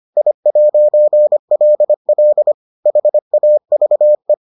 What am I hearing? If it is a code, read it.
Morse code, heard as a keyed tone, about 25 wpm, 600 Hz. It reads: I'LL HAVE